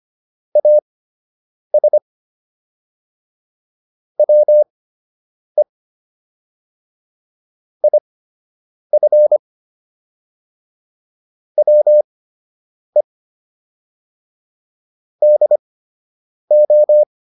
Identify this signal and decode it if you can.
Morse code, heard as a keyed tone, about 25 wpm, spaced as for 8 wpm, 600 Hz. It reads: AS WE IF WE DO